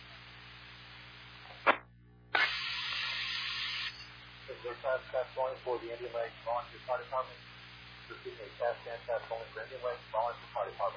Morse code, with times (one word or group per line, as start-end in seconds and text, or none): none